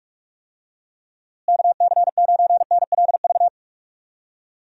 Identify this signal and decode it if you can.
Morse code, heard as a keyed tone, about 46 wpm, 700 Hz. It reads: KC9NLV
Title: USA Call Signs - 6 Characters - Repeated 3x - 45wpm